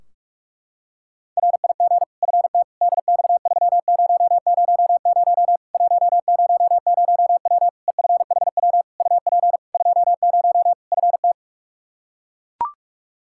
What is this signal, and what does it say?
Morse code, heard as a keyed tone, about 42 wpm, 700 Hz. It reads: RIG FT DX3000 100W EFHW UP 20 FT
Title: K1USN SST QSO Practice 45wpm